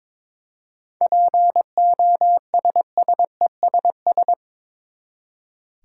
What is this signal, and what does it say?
Morse code, heard as a keyed tone, about 21 wpm, 700 Hz. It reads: POSSESS
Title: Seven Letter Words - 22wpm